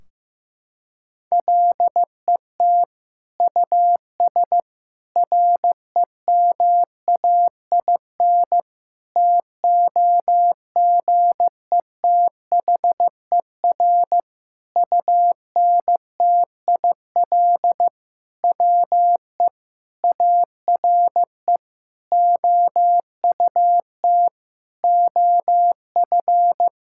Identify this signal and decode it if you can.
Morse code, heard as a keyed tone, about 15 wpm, 700 Hz. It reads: LET US REMAIN TOGETHER UNTIL WE ARE OUT OF
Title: Sentences from Top 1600 Words 15wpm